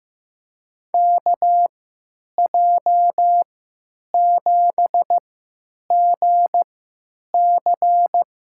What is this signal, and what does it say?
Morse code, heard as a keyed tone, about 16 wpm, spaced as for 9 wpm, 700 Hz. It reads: KJ7GC
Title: USA Call Signs - 3x Character Spacing - 15wpm